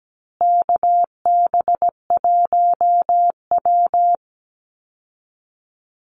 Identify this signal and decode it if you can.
Morse code, heard as a keyed tone, about 17 wpm, 700 Hz. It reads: KB1W
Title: Call Signs Repeated 3x - 17wpm